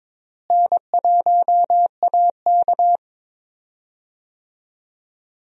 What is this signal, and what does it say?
Morse code, heard as a keyed tone, about 22 wpm, 700 Hz. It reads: N1AK
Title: USA Call Signs - 4 Characters - Repeated 3x - 22wpm